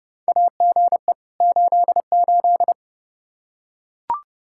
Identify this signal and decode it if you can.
Morse code, heard as a keyed tone, about 32 wpm, 700 Hz. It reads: AGE 88
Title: Practice QSO - Exchange 3 - More Info - 30wpm